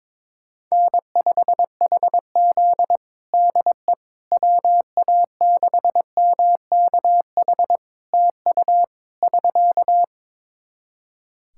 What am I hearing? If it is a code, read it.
Morse code, heard as a keyed tone, about 22 wpm, 700 Hz. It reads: N5HZ DE WA6MKH TU <SK>
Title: Practice QSO - Exchange 4 - Ending - 22wpm